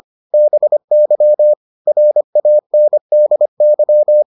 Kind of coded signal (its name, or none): Morse code